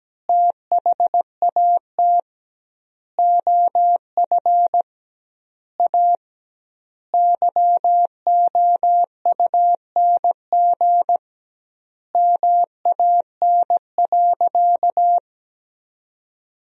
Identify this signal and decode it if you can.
Morse code, heard as a keyed tone, about 17 wpm, 700 Hz. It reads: THAT OF A YOUNG MAN.
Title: 2x Word-Spacing - Sentences from Top 500 Words - 17wpm